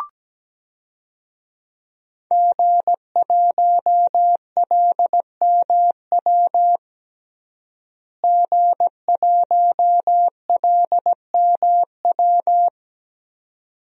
Morse code, heard as a keyed tone, about 17 wpm, 700 Hz. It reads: G1LMW G1LMW
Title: DXCC-100 Call Signs - 5 Characters - Repeated 2x - 17wpm